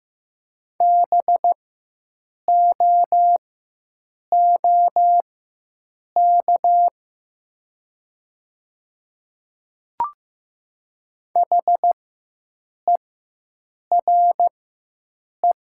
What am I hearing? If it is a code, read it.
Morse code, heard as a keyed tone, about 15 wpm, spaced as for 7 wpm, 700 Hz. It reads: BOOK HERE
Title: 4x Character Spacing - Top 300 Words - 15wpm